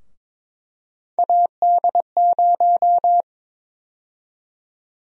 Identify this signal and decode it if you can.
Morse code, heard as a keyed tone, about 22 wpm, 700 Hz. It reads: AD0
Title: USA Call Sign Prefixes - Repeated 2x - 22wpm